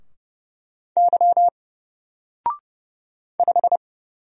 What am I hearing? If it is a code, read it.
Morse code, heard as a keyed tone, about 29 wpm, 700 Hz. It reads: Y 5
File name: Single-Letter-Number-30wpm